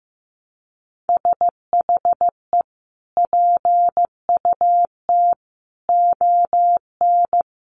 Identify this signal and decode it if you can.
Morse code, heard as a keyed tone, about 15 wpm, 700 Hz. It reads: SHE PUT ON
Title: Sentences from Top 1200 Words 15wpm